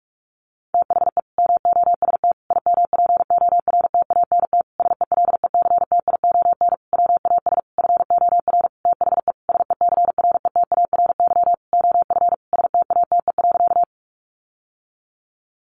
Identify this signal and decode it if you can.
Morse code, heard as a keyed tone, about 44 wpm, 700 Hz. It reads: THE MOST IMPORTANT SELECTION WAS FOR THE SECRETARY OF STATE.
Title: Sentences from Top 1400 Words 45wpm